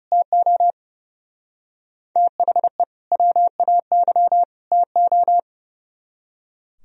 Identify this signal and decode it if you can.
Morse code, heard as a keyed tone, about 37 wpm, 700 Hz. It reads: TO THE WAY TO
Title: Speed-Racing - Sets of 3 Words 30wpm